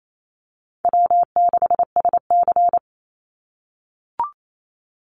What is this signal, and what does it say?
Morse code, heard as a keyed tone, about 28 wpm, 700 Hz. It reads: W6SC
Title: Easy Call Signs - 28wpm